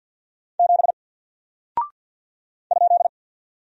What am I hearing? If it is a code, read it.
Morse code, heard as a keyed tone, about 48 wpm, 700 Hz. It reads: 7 ?
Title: Single Character (Letter:Number:Punctuation:Prosign) 50wpm